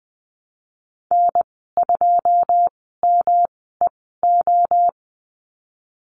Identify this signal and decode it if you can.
Morse code, heard as a keyed tone, about 20 wpm, spaced as for 14 wpm, 700 Hz. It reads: N2MEO
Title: 128 Lesson 21g - Number 6 - Call Signs 20wpm